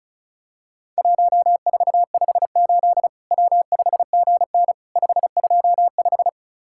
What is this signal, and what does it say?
Morse code, heard as a keyed tone, about 36 wpm, 700 Hz. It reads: 1458 W5GN 525
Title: Worked All Europe DX Contest - QTC Exchange - 35wpm